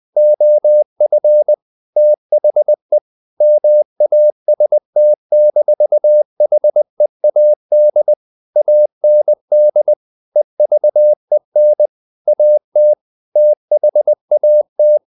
Morse code, heard as a keyed tone, about 20 wpm, 600 Hz. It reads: OF THE MAST-HEAD AND EVEN AT THAT
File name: the old man and the sea part 2 20wpm